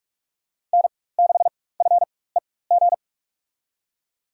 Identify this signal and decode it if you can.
Morse code, heard as a keyed tone, about 46 wpm, spaced as for 21 wpm, 700 Hz. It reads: N6FEG
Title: USA Call Signs - 4x Character Spacing - 45wpm